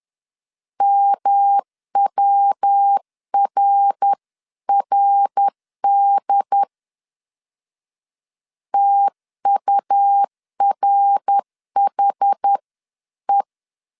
Morse code, heard as a keyed tone, about 10 wpm, 800 Hz. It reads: MWRRD TURHE
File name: 10wpm